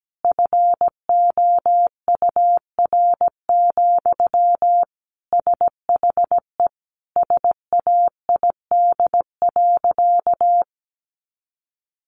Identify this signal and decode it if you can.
Morse code, heard as a keyed tone, about 17 wpm, 700 Hz. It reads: FOUR, SHE SAID.
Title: Sentences from Top 300 Words 17wpm